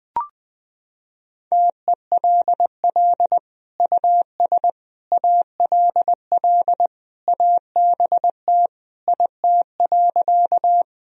Morse code, heard as a keyed tone, about 21 wpm, 700 Hz. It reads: TELL US ALL ABT IT.
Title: Sentences from Top 200 Words and Abbreviations 20wpm